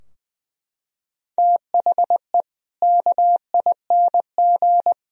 Morse code, heard as a keyed tone, about 20 wpm, 700 Hz. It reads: THE KING
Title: Sentences from Top 1000 Words - Review Part 4 - 20wpm